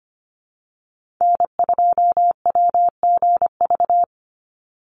Morse code, heard as a keyed tone, about 25 wpm, 700 Hz. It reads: N2WGV
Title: USA Call Signs 25wpm